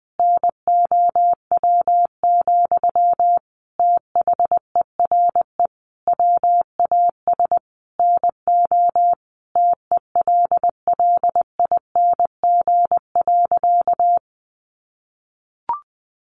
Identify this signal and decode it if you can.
Morse code, heard as a keyed tone, about 20 wpm, 700 Hz. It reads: NOW, THERE WAS NO TELLING.
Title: A Sound Of Thunder 20wpm